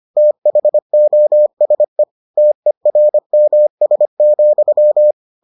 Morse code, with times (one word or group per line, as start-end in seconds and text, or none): 0.2-2.0 THOSE
2.4-5.1 TERMS,